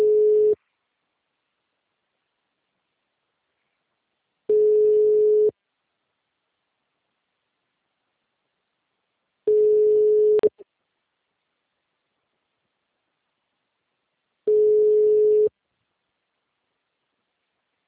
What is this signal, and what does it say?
Ringback tone (Europe, 425 Hz), with speech.
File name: out- -2016- - - .116